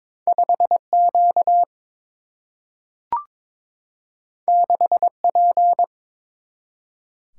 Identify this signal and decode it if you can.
Morse code, heard as a keyed tone, about 23 wpm, 700 Hz. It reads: 5 Q 6 P
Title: Single-Letter Single-Number 22wpm